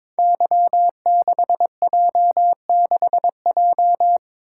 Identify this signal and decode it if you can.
Morse code, heard as a keyed tone, about 23 wpm, 700 Hz. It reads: Y6J6J
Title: Code Groups - 5 Letter:Numbers - 22wpm